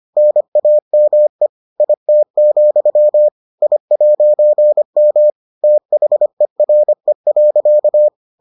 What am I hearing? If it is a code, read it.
Morse code, heard as a keyed tone, about 25 wpm, 600 Hz. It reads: NAME IT, I'M THERE.